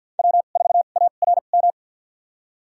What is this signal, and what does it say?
Morse code, heard as a keyed tone, about 52 wpm, spaced as for 36 wpm, 700 Hz. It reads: W4ARM
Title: USA Call Signs - 5 Characters - 2x Character Spacing - 50wpm